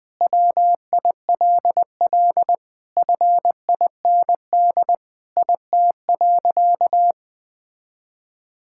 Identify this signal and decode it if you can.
Morse code, heard as a keyed tone, about 20 wpm, 700 Hz. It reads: WILL FIND IT.
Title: Sentences with 3-Word Phrases - 20wpm